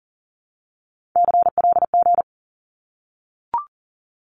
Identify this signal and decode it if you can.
Morse code, heard as a keyed tone, about 38 wpm, 700 Hz. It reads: CLG
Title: Abbreviations 40wpm